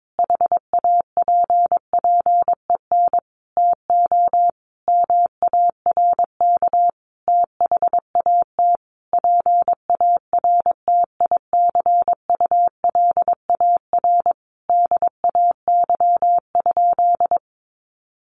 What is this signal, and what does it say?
Morse code, heard as a keyed tone, about 22 wpm, 700 Hz. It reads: HAPPEN TO MARK THAT PARTICULAR DAY?
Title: Sentences from Top 1400 Words 22wpm